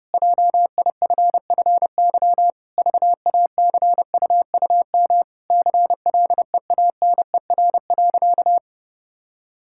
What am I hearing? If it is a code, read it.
Morse code, heard as a keyed tone, about 32 wpm, 700 Hz. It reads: JIFFY VACUUM CLEANER.
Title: Pangram Challenge v2 - 30wpm